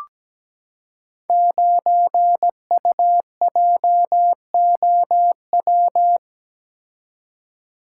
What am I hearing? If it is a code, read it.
Morse code, heard as a keyed tone, about 17 wpm, 700 Hz. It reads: 9UJOW
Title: Code Groups - 5 Letter:Numbers - 17wpm